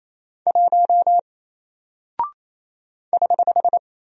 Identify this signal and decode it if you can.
Morse code, heard as a keyed tone, about 28 wpm, 700 Hz. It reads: 1 <HH>
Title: Single Character (Letter:Number:Punctuation:Prosign) Alt 28wpm